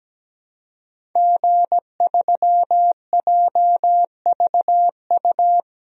Morse code, heard as a keyed tone, about 17 wpm, 700 Hz. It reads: G3JVU
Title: DXCC-100 Call Signs - 5 Characters - Repeated 2x - 17wpm